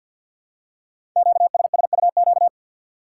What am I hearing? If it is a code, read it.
Morse code, heard as a keyed tone, about 50 wpm, 690 Hz. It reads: QSSUQ
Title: Code Groups - 5 Letters - 50wpm